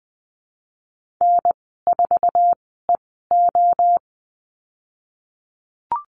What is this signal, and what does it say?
Morse code, heard as a keyed tone, about 19 wpm, spaced as for 14 wpm, 700 Hz. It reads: N4EO
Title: 055 Lesson 10g - Letter r - Call Signs 20wpm